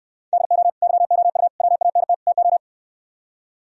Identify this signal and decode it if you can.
Morse code, heard as a keyed tone, about 67 wpm, 700 Hz. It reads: DO YOU WANT TO